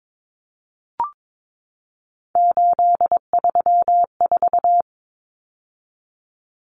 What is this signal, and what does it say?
Morse code, heard as a keyed tone, about 22 wpm, 700 Hz. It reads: 834
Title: Common Cut Numbers - 3-Digit 22wpm